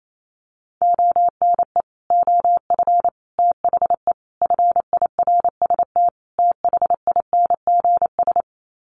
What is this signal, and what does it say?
Morse code, heard as a keyed tone, about 28 wpm, 700 Hz. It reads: ONE OF THE FIRST THINGS
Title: Sets of 5 Words Encore 28wpm